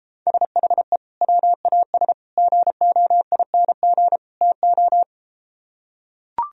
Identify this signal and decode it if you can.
Morse code, heard as a keyed tone, about 33 wpm, 700 Hz. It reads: SHE WAS GOING TO